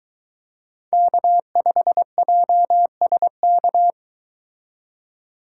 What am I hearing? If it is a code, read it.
Morse code, heard as a keyed tone, about 24 wpm, 700 Hz. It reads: K5JSK